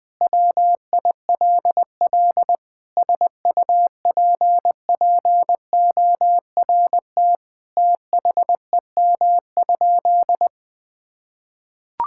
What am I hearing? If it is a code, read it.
Morse code, heard as a keyed tone, about 20 wpm, 700 Hz. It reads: WILL SUPPORT THEM?
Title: Sentences from Top 1000 Words - Review Part 5 - 20wpm